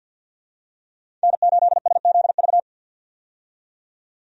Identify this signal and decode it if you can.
Morse code, heard as a keyed tone, about 46 wpm, 700 Hz. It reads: N8SZV
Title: USA Call Signs - 5 Characters - 50wpm